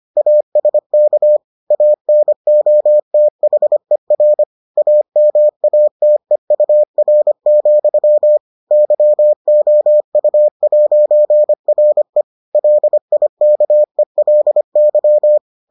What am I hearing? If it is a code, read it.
Morse code, heard as a keyed tone, about 25 wpm, 600 Hz. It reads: ASK ANOTHER AMATEUR, YOU'RE LIKELY